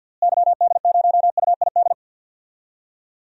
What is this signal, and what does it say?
Morse code, heard as a keyed tone, about 52 wpm, 700 Hz. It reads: YD0UID
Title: DXCC-100 Call Signs - 6 Characters - Repeated 3x - 50wpm